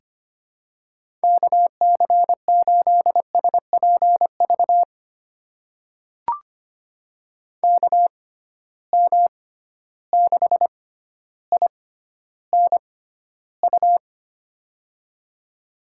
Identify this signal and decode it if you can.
Morse code, heard as a keyed tone, about 25 wpm, spaced as for 13 wpm, 700 Hz. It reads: KC8SPV KM6INU
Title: USA Call Signs - 6 Characters - 6x Character Spacing - 25wpm